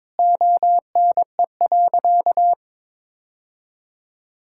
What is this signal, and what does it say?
Morse code, heard as a keyed tone, about 22 wpm, 700 Hz. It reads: ONE.
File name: practice / 22wpm / Sentences with 2-Word Phrases - 22wpm